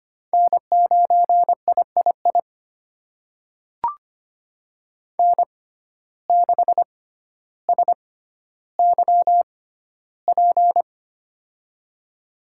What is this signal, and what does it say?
Morse code, heard as a keyed tone, about 25 wpm, spaced as for 13 wpm, 700 Hz. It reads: N9III N6SYP
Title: USA Call Signs - 5 Characters - 6x Character Spacing - 25wpm